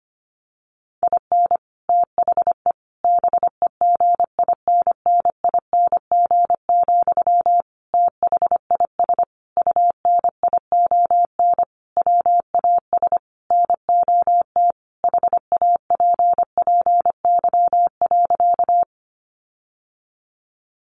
Morse code, heard as a keyed tone, about 25 wpm, 700 Hz. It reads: IN THE BEGINNING, THIS UNION WAS NOT HAPPY.
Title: Sentences from Top 1400 Words 25wpm